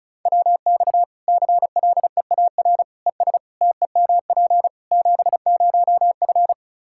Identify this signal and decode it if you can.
Morse code, heard as a keyed tone, about 36 wpm, 700 Hz. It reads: WX CLEAR ES TEMP 70F
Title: K1USN SST QSO Practice 35wpm